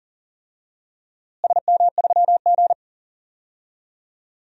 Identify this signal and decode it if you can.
Morse code, heard as a keyed tone, about 40 wpm, 700 Hz. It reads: SM3G
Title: DXCC-100 Call Signs - 4 Characters - Repeated 2x - 40wpm